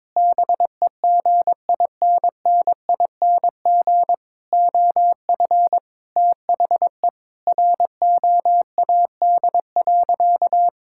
Morse code, heard as a keyed tone, about 22 wpm, 700 Hz. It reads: BEGINNING OF THE ROAD.
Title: Sentences from Top 1000 Words - Review Part 5 - 22wpm